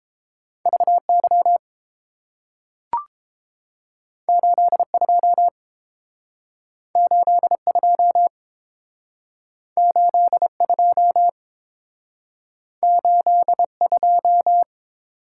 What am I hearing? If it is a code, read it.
Morse code, heard as a keyed tone, about 30 wpm, 700 Hz. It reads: V Y 8 2 8 2 8 2 8 2